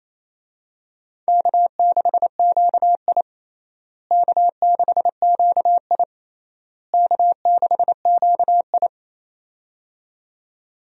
Morse code, heard as a keyed tone, about 28 wpm, 700 Hz. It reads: K6QI K6QI K6QI
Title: USA Call Signs - 4 Characters - Repeated 3x - 28wpm